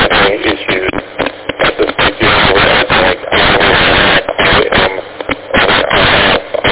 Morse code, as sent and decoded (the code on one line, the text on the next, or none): none